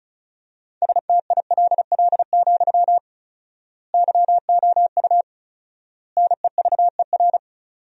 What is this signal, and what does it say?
Morse code, heard as a keyed tone, about 34 wpm, 700 Hz. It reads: STILL, YOU NEVER
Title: 4x Word-Spacing - Sentences from Top 300 Words - 35wpm